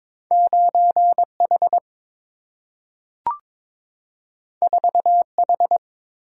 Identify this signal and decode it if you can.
Morse code, heard as a keyed tone, about 22 wpm, 700 Hz. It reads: 9H 4H
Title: Single-Letter Single-Number 22wpm